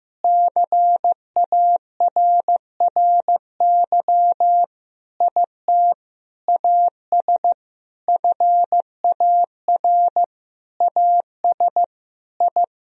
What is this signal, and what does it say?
Morse code, heard as a keyed tone, about 16 wpm, 700 Hz. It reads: CARRY IT AS FAR AS I
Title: Sentences from Top 1000 Words - Review Part 1 - 15wpm